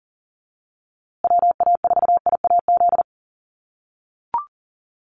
Morse code, heard as a keyed tone, about 40 wpm, 700 Hz. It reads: WA4IAZ
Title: USA Call Signs Repeated 3x 40wpm